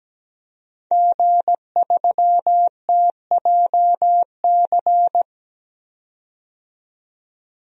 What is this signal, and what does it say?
Morse code, heard as a keyed tone, about 17 wpm, 700 Hz. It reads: G3TJC
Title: DXCC-100 Call Signs - 5 Characters - 17wpm